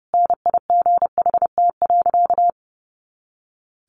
Morse code, heard as a keyed tone, about 30 wpm, 700 Hz. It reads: NIGHT.
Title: Sentences from Top 300 Words 30wpm